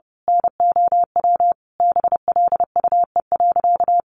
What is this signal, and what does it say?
Morse code, heard as a keyed tone, about 30 wpm, 700 Hz. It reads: NOW BLUE.
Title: A Sound Of Thunder 30wpm